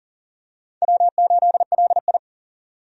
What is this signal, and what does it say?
Morse code, heard as a keyed tone, about 37 wpm, 700 Hz. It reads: W8LI
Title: USA Call Signs - 4 Characters - 40wpm